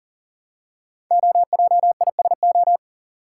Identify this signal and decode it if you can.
Morse code, heard as a keyed tone, about 42 wpm, 700 Hz. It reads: OJISO
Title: Code Groups - 5 Letters - 40wpm